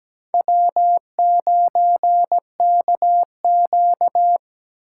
Morse code, heard as a keyed tone, about 17 wpm, 700 Hz. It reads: W9KQ
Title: USA Call Signs - 4 Characters - 6x Character Spacing - 17wpm